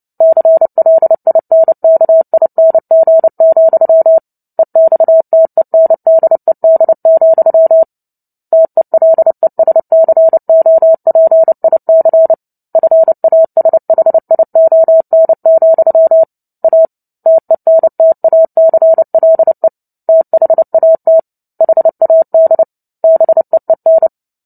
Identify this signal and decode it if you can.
Morse code, heard as a keyed tone, about 30 wpm, 650 Hz. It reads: CLINKING, EXTENDED, TELESCOPIC FASHION, A TENTACLE THAT HAD BEEN
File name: Book Two / 03-Chapter 3 - 29 WPM